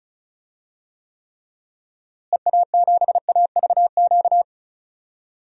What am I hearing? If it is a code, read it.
Morse code, heard as a keyed tone, about 36 wpm, 700 Hz. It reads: EA7AVQ
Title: DXCC-100 Call Signs - 6 Characters - Repeated 3x - 35wpm